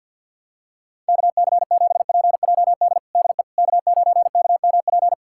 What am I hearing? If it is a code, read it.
Morse code, heard as a keyed tone, about 50 wpm, 700 Hz. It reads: KC7PJD DE K9KMP
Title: USA Callsign DE Callsign 50wpm